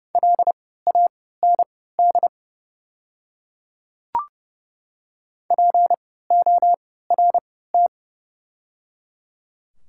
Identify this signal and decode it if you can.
Morse code, heard as a keyed tone, about 30 wpm, spaced as for 17 wpm, 700 Hz. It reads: LAND PORT
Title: 3x Character Spacing - Top 300 Words - 30wpm